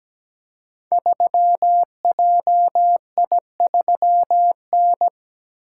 Morse code, heard as a keyed tone, about 17 wpm, 700 Hz. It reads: 3JI3N
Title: Code Groups - 5 Letter:Numbers - 17wpm